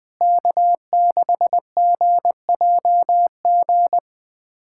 Morse code, heard as a keyed tone, about 20 wpm, 700 Hz. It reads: K6GJG
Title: USA Callsign DE Callsign - 5 Characters - 20wpm